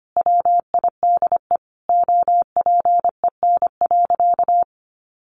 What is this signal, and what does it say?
Morse code, heard as a keyed tone, about 25 wpm, 700 Hz. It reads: WIDE OPEN.
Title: Sentences from Top 900 Words 25wpm